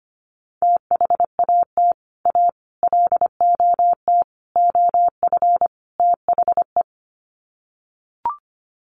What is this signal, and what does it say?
Morse code, heard as a keyed tone, about 25 wpm, 700 Hz. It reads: THAT A LOT OF THE
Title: Sets of 5 Words Encore2 25wpm